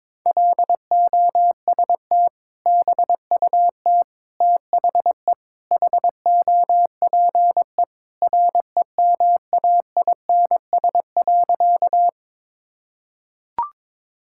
Morse code, heard as a keyed tone, about 22 wpm, 700 Hz. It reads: LOST BUT THE HOPE REMAINS.
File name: practice / 22wpm / Sentences with 2-Word Phrases - 22wpm